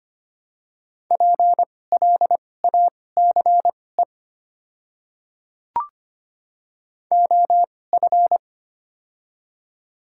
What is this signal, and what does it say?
Morse code, heard as a keyed tone, about 25 wpm, spaced as for 18 wpm, 700 Hz. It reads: PLACE OF
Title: 2x Character Spacing - Top 500 Words - 25wpm